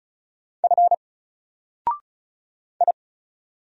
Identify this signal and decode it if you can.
Morse code, heard as a keyed tone, about 36 wpm, 700 Hz. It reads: F I
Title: Single Character (Letter:Number:Punctuation:Prosign) Alt 35wpm